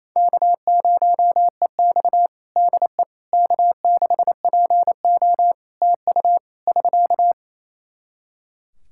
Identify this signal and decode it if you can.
Morse code, heard as a keyed tone, about 28 wpm, 700 Hz. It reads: K0EX DE K6PO TU <SK>
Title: Practice QSO - Exchange 4 - Ending - 28wpm